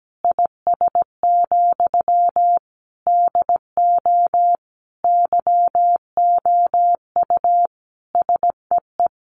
Morse code, heard as a keyed tone, about 17 wpm, 700 Hz. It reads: IS, DO YOU SEE
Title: Sentences from Top 100 Words 17wpm